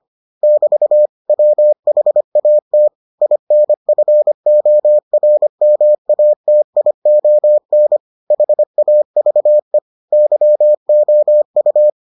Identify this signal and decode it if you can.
Morse code, heard as a keyed tone, about 25 wpm, 600 Hz. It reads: <BT> WHAT INFORMATION HAVE YOU